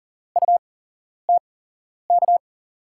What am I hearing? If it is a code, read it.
Morse code, heard as a keyed tone, about 39 wpm, spaced as for 11 wpm, 700 Hz. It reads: UTK
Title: USA Call Signs - 5 Characters - 8x Character Spacing - 40wpm